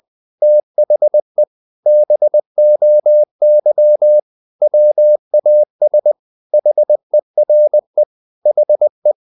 Morse code, heard as a keyed tone, about 20 wpm, 600 Hz. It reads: THE BOY WAS HERE HE
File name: the old man and the sea part 2 20wpm